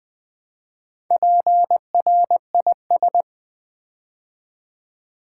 Morse code, heard as a keyed tone, about 21 wpm, 700 Hz. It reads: PRIS
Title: Build A Word - 2 Letters At A Time - 8-Letter Words - 20wpm